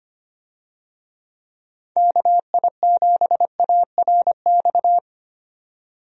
Morse code, heard as a keyed tone, about 25 wpm, 700 Hz. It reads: KI7ARX
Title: USA Call Signs - 6 Characters - 25wpm